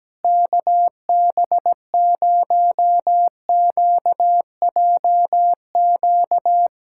Morse code, heard as a keyed tone, about 17 wpm, 700 Hz. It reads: KB0QJQ
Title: SOTA QSO Practice 17wpm